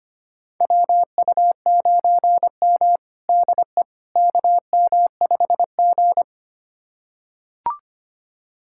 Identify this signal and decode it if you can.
Morse code, heard as a keyed tone, about 25 wpm, 700 Hz. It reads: WU9M DE KM5G
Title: USA Callsign DE Callsign - 4 Characters - 25wpm